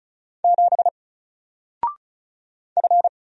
Morse code, heard as a keyed tone, about 36 wpm, 700 Hz. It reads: 7 F
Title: Single Letter-Number 35wpm